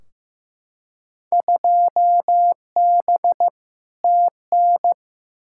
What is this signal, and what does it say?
Morse code, heard as a keyed tone, about 15 wpm, 700 Hz. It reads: 2B TN
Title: Field Day Exchanges Encore - 15wpm